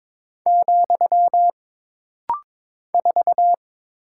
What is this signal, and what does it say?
Morse code, heard as a keyed tone, about 22 wpm, 700 Hz. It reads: , 4
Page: This rapid-fire timing is geared to push your ICR (Instant Character Recognition) skills.